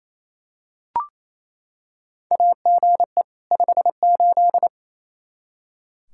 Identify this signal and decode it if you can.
Morse code, heard as a keyed tone, about 28 wpm, 700 Hz. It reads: AGE 58
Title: Practice QSO - Exchange 3 - More Info - 28wpm